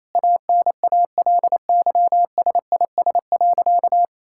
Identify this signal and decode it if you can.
Morse code, heard as a keyed tone, about 28 wpm, 700 Hz. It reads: ANALYSIS.
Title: Sentences from Top 1000 Words - Review Part 5 - 28wpm